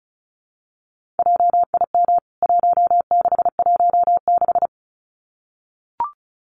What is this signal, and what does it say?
Morse code, heard as a keyed tone, about 34 wpm, 700 Hz. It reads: JIM 1616
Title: CWT Exchange 35wpm